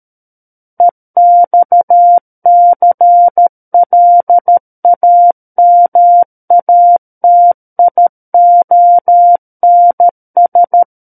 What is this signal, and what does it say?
Morse code, heard as a keyed tone, about 13 wpm, 700 Hz. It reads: EXCLAMATIONS
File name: Book One / 04-Chapter 4 - 13 WPM